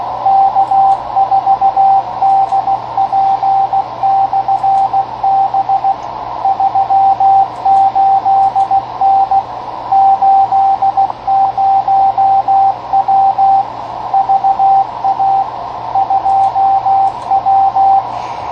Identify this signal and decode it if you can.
Morse code, heard as a keyed tone, about 16 wpm, 780 Hz, with radio noise. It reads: K4DP/B 38N 80W VA 2W
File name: 10m/K4DP 28.234 2watt